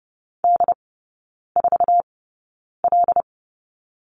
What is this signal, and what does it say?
Morse code, heard as a keyed tone, about 30 wpm, spaced as for 9 wpm, 690 Hz. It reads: D4L
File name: Lesson 17e - Letter u - Three Characters 30wpm